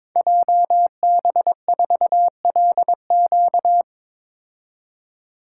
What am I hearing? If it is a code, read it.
Morse code, heard as a keyed tone, about 22 wpm, 700 Hz. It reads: JB4LQ
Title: Code Groups - 5 Letter:Numbers - 22wpm